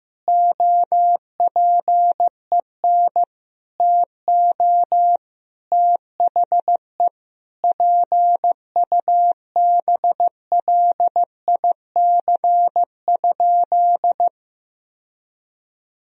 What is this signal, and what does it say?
Morse code, heard as a keyed tone, about 15 wpm, 700 Hz. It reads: OPEN TO THE PUBLIC?
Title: Sentences from Top 1000 Words - Review Part 3 - 15wpm